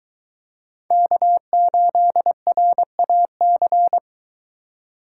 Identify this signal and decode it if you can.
Morse code, heard as a keyed tone, about 24 wpm, 700 Hz. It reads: K8RAC